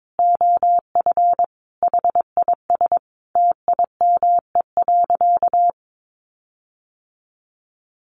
Morse code, heard as a keyed tone, about 22 wpm, 700 Hz. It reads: OF HIS TIME.
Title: Sentences from Top 1400 Words 22wpm